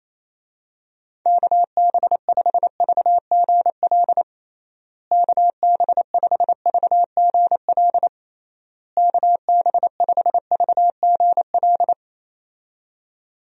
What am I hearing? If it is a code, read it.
Morse code, heard as a keyed tone, about 28 wpm, 700 Hz. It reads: KB5VGL KB5VGL KB5VGL
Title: USA Call Signs - 6 Characters - Repeated 3x - 28wpm